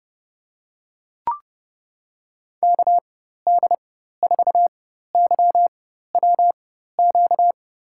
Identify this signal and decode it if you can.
Morse code, heard as a keyed tone, about 31 wpm, spaced as for 14 wpm, 700 Hz. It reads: KD4YWQ
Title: USA Call Signs - 6 Characters - 4x Character Spacing - 30wpm